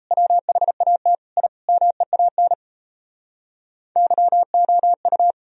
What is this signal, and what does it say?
Morse code, heard as a keyed tone, about 38 wpm, 700 Hz. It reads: WHAT I MEAN YOU